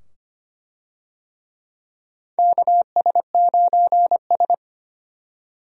Morse code, heard as a keyed tone, about 25 wpm, 700 Hz. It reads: KS9S
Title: USA Call Signs - 4 Characters - 25wpm